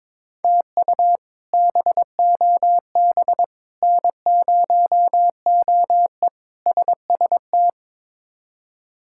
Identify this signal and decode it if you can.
Morse code, heard as a keyed tone, about 22 wpm, 700 Hz. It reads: TU BOB N0OE SST
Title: K1USN SST QSO Practice - 22wpm